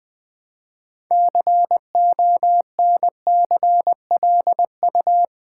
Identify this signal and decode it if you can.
Morse code, heard as a keyed tone, about 20 wpm, 700 Hz. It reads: CONCLU
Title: Build A Word - 2 Letters At A Time - 8-Letter Words - 20wpm